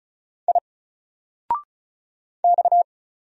Morse code, heard as a keyed tone, about 36 wpm, 700 Hz. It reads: I X
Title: Single Character (Letter:Number:Punctuation:Prosign) 35wpm